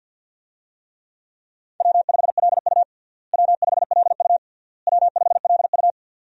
Morse code, heard as a keyed tone, about 46 wpm, 700 Hz. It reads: W5LU W5LU W5LU
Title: USA Call Signs - 4 Characters - Repeated 3x - 50wpm